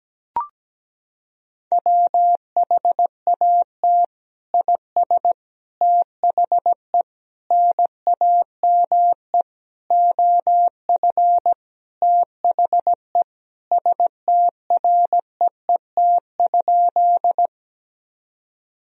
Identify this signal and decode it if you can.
Morse code, heard as a keyed tone, about 17 wpm, 700 Hz. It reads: WHAT IS THE NAME OF THE STREET?
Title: Sentences from Top 1000 Words - Review Part 3 - 17wpm